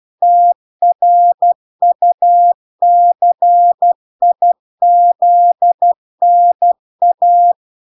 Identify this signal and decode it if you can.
Morse code, heard as a keyed tone, about 12 wpm, 700 Hz. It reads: TRUCIZNA